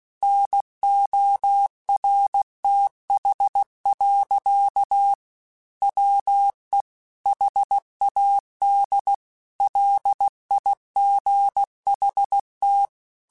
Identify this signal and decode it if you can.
Morse code, heard as a keyed tone, about 16 wpm, 800 Hz. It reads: NORTH. WE HAD LIGHT
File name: Weather Sets / Weather Sets-mp3s / Weather 4 28 @15